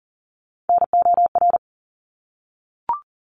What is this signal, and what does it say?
Morse code, heard as a keyed tone, about 40 wpm, 700 Hz. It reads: NOR
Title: Top 1251-1500 Words 40wpm